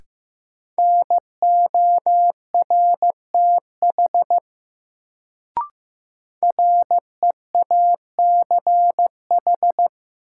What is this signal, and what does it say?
Morse code, heard as a keyed tone, about 15 wpm, 700 Hz. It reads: NORTH REACH
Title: Sentences from Top 400 Words - New Word Drill - 15wpm